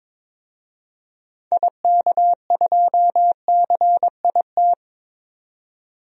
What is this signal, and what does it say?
Morse code, heard as a keyed tone, about 22 wpm, 700 Hz. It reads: IK2CIT
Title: DXCC-100 Call Signs - 6 Characters - 22wpm